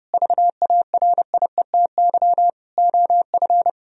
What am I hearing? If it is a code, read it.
Morse code, heard as a keyed tone, about 30 wpm, 700 Hz. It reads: VARIETY OF
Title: Sentences from Top 1000 Words - Review Part 4 - 30wpm